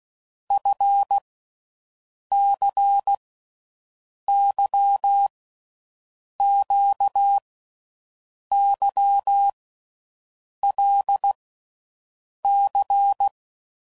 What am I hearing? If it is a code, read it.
Morse code, heard as a keyed tone, about 16 wpm, 800 Hz. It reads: F C Y Q Y L C
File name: Groups / Basic Letter Groups / LFCYQ